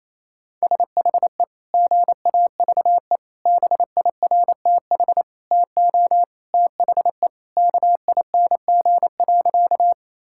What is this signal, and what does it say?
Morse code, heard as a keyed tone, about 28 wpm, 700 Hz. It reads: SHE GAVE BIRTH TO THE KING.
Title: Sentences from Top 1000 Words - Review Part 5 - 28wpm